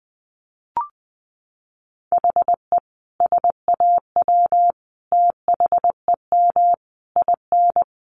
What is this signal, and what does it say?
Morse code, heard as a keyed tone, about 20 wpm, 700 Hz. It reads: HE SAW THEM IN
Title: Sentences from Top 1300 Words 20wpm